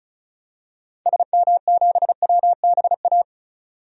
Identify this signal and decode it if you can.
Morse code, heard as a keyed tone, about 36 wpm, 700 Hz. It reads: SM7WBA
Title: DXCC-100 Call Signs - 6 Characters - 35wpm